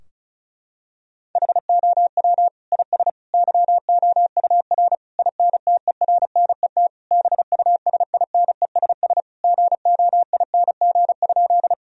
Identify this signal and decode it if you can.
Morse code, heard as a keyed tone, about 35 wpm, 700 Hz. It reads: HOW IS YOUR INTERNET BUSINESS GOING?
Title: Sentences from Top 1000 Words - Review Part 4 - 35wpm